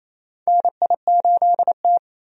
Morse code, heard as a keyed tone, about 28 wpm, 700 Hz. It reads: NI8T
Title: USA Call Signs - 4 Characters - 28wpm